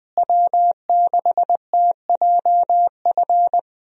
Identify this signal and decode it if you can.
Morse code, heard as a keyed tone, about 20 wpm, 700 Hz. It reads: W6TJF
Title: USA Call Signs - 5 Characters - Repeated 3x - 20wpm